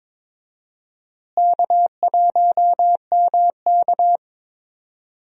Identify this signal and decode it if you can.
Morse code, heard as a keyed tone, about 22 wpm, 700 Hz. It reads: K1MK
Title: USA Call Signs - 4 Characters - Repeated 3x - 22wpm